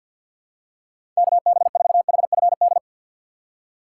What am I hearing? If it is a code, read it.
Morse code, heard as a keyed tone, about 52 wpm, 700 Hz. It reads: KB4HFD
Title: USA Call Signs - 2x Character Spacing - 50wpm